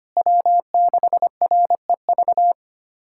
Morse code, heard as a keyed tone, about 25 wpm, 700 Hz. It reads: W6REV
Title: USA Callsign DE Callsign 25wpm